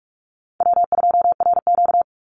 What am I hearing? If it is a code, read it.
Morse code, heard as a keyed tone, about 44 wpm, 700 Hz. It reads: W2RQ
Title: Call Signs Repeated 2x - 45wpm